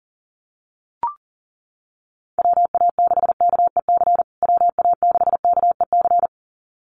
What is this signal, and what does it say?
Morse code, heard as a keyed tone, about 40 wpm, 700 Hz. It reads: WA6KEC WA6KEC
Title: USA Call Signs Repeated 2x 40wpm